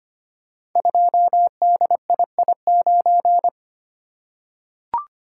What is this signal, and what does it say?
Morse code, heard as a keyed tone, about 25 wpm, 700 Hz. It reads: 2DII9
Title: Code Groups - 5 Letter:Numbers - 25wpm